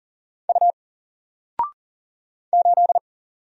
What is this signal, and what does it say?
Morse code, heard as a keyed tone, about 39 wpm, 700 Hz. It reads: U 8
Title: Single Letter-Number 40wpm